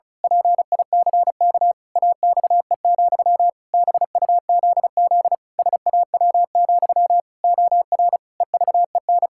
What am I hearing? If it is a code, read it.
Morse code, heard as a keyed tone, about 34 wpm, 700 Hz. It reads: PICK AXE, BUZZ SAW, OR EVEN
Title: Pangram Challenge v2 - 35wpm